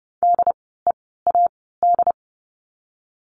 Morse code, heard as a keyed tone, about 30 wpm, 700 Hz. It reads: D E A D
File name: Lesson 16f - Letter c - Words 30wpm